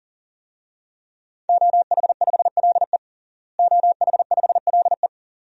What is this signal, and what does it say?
Morse code, heard as a keyed tone, about 40 wpm, 700 Hz. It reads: OH5LE OH5LE
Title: DXCC-100 Call Signs - 5 Characters - Repeated 2x - 40wpm